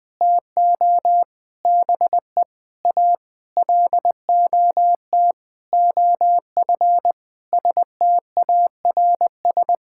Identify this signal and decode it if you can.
Morse code, heard as a keyed tone, about 20 wpm, 700 Hz. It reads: TO BE A LOT OF STARS
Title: Sentences with 8-Word Phrases - Spoken Beginning - 20wpm